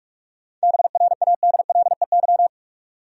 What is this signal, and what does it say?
Morse code, heard as a keyed tone, about 46 wpm, 700 Hz. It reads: BRADLEY
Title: USA Common Names - 45wpm